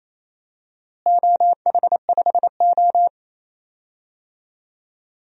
Morse code, heard as a keyed tone, about 27 wpm, 700 Hz. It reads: OH5O
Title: DXCC-100 Call Signs - 4 Characters - Repeated 2x - 28wpm